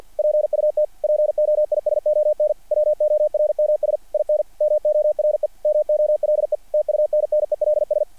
Morse code, heard as a keyed tone, about 48 wpm, 660 Hz. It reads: PUT POISON WORMS IN MORE MOLE TUNNELS
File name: archived / CRSnet